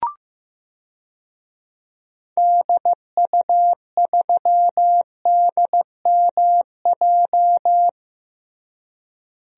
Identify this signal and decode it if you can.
Morse code, heard as a keyed tone, about 15 wpm, 700 Hz. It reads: DU3DMJ
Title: DXCC-100 Call Signs - 6 Characters - Repeated 2x - 15wpm